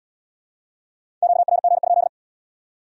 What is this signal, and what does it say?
Morse code, heard as a keyed tone, about 69 wpm, 700 Hz. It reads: QRG?